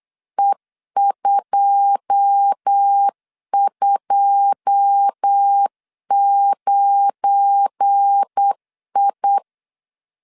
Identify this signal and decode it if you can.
Morse code, heard as a keyed tone, about 8 wpm, 800 Hz. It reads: E229I
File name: LEZIONI / 9 2 8wpm 3437